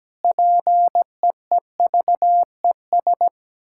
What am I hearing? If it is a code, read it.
Morse code, heard as a keyed tone, about 17 wpm, 700 Hz. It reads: PEEVES
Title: Pangram Challenge v2 - 17wpm